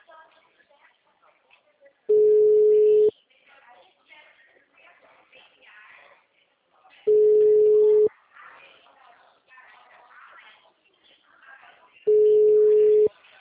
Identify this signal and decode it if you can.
Ringback tone (Europe, 425 Hz), with speech.